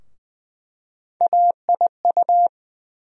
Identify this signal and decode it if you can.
Morse code, heard as a keyed tone, about 20 wpm, 700 Hz. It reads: AIU
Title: USA Call Sign Postfixes - 3 Letters 20wpm